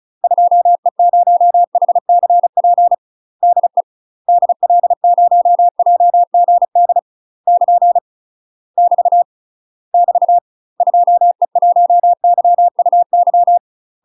Morse code, heard as a keyed tone, about 36 wpm, 700 Hz. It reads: 2E0HCP DE DL0JGD <KN> <BT> <BT> 2E1YUY
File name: UK MASTER QSOs TRAINING / 35WPM / UK MASTER QSOs 0031